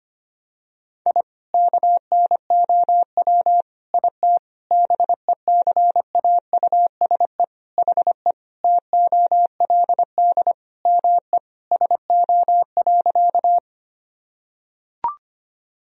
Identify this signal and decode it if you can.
Morse code, heard as a keyed tone, about 25 wpm, 700 Hz. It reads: I KNOW IT BECAUSE HE TOLD ME SO.
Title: Sentences from Top 1000 Words - Review Part 3 - 25wpm